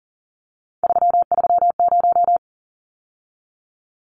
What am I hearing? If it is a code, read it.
Morse code, heard as a keyed tone, about 40 wpm, 700 Hz. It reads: 330
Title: Numbers - 3-Digit 40wpm